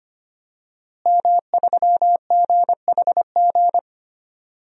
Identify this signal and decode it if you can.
Morse code, heard as a keyed tone, about 25 wpm, 700 Hz. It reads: M3GHG
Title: DXCC-100 Call Signs - 5 Characters - Repeated 2x - 25wpm